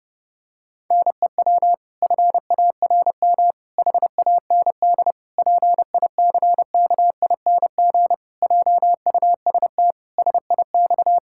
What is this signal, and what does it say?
Morse code, heard as a keyed tone, about 30 wpm, 700 Hz. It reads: NEW FARM HAND PICKING JUST SIX
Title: Pangram Challenge v2 - 30wpm